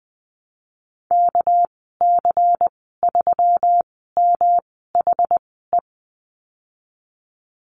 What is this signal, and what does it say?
Morse code, heard as a keyed tone, about 20 wpm, spaced as for 14 wpm, 700 Hz. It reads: KC3MHE
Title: 213 Lesson 34g - Number 8 - Call Signs 20wpm